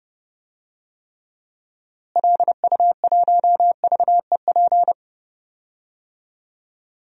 Morse code, heard as a keyed tone, about 30 wpm, 700 Hz. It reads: LU1VEP
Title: DXCC-100 Call Signs - 6 Characters - Repeated 2x - 30wpm